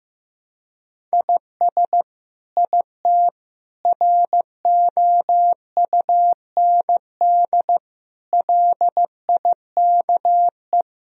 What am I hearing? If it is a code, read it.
Morse code, heard as a keyed tone, about 16 wpm, 700 Hz. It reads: IS IT ROUND LIKE
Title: Sentences from Top 1000 Words - Review Part 3 - 15wpm